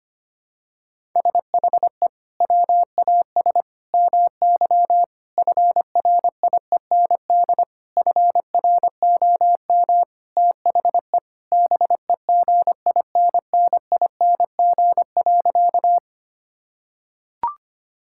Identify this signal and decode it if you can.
Morse code, heard as a keyed tone, about 25 wpm, 700 Hz. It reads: SHE WAS MY FRIEND FROM THE BEGINNING.
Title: Sentences from Top 1000 Words - Review Part 5 - 25wpm